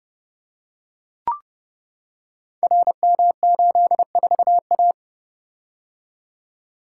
Morse code, heard as a keyed tone, about 30 wpm, 700 Hz. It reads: RM84A
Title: Code Groups - 5 Letter:Numbers - 30wpm